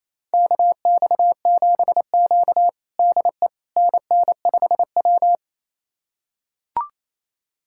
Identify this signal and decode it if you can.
Morse code, heard as a keyed tone, about 27 wpm, 700 Hz. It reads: KX7Q DE NN5W
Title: USA Callsign DE Callsign - 4 Characters - 28wpm